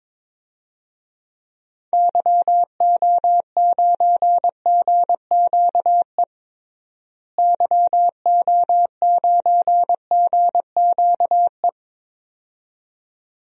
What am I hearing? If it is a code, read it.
Morse code, heard as a keyed tone, about 22 wpm, 700 Hz. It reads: YO9GQE YO9GQE
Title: DXCC-100 Call Signs - 6 Characters - Repeated 2x - 22wpm